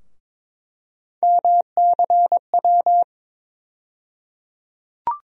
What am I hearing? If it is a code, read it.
Morse code, heard as a keyed tone, about 22 wpm, 700 Hz. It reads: MCW
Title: USA Call Sign Postfixes - 3 Letters 22wpm